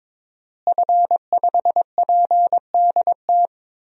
Morse code, heard as a keyed tone, about 22 wpm, 700 Hz. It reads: F5PDT
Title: DXCC-100 Call Signs - 5 Characters - Repeated 2x - 22wpm